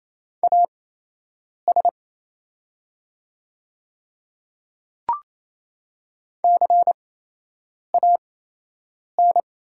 Morse code, heard as a keyed tone, about 28 wpm, spaced as for 8 wpm, 700 Hz. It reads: AS CAN
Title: 8x Character Spacing - Top 100 Words - 28wpm